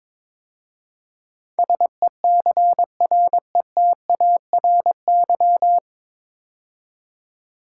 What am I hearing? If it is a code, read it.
Morse code, heard as a keyed tone, about 22 wpm, 700 Hz. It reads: SECRETARY
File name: 9 Letter Words - First 3 Letters Given - 22wpm